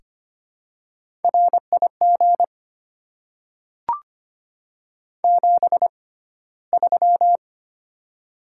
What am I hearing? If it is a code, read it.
Morse code, heard as a keyed tone, about 25 wpm, spaced as for 15 wpm, 700 Hz. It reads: RIG 73
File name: Top 20 QSO Elements - 6x Character Spacing - 25wpm